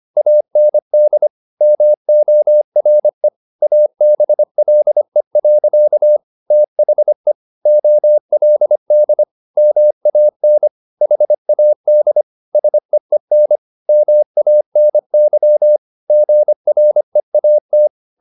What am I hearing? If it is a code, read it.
Morse code, heard as a keyed tone, about 25 wpm, 600 Hz. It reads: AND MORE ABLE. THE OLD MAN HAD SEEN MANY GREAT
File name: the old man and the sea part 1 25wpm